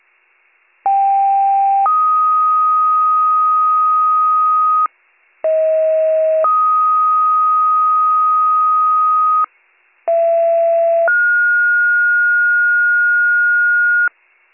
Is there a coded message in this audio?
Yes: two-tone page from 0.9 to 14.1 s, heard as a one-second tone, then a three-second tone, about 790 then 1250 Hz, about 640 then 1250 Hz, about 670 then 1430 Hz.